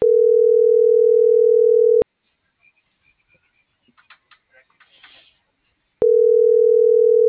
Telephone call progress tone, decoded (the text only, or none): ringback tone